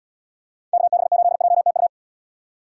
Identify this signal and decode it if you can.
Morse code, heard as a keyed tone, about 78 wpm, 700 Hz. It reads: KB8JTU